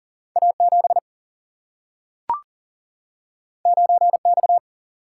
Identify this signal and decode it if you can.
Morse code, heard as a keyed tone, about 40 wpm, 700 Hz. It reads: A 7 9 X
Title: Single-Letter Single-Number 40wpm